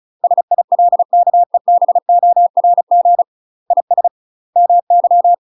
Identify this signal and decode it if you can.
Morse code, heard as a keyed tone, about 36 wpm, 700 Hz. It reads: SILKEBORG IS MY
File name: UK MASTER QSOs TRAINING / 35WPM / UK MASTER QSOs 0012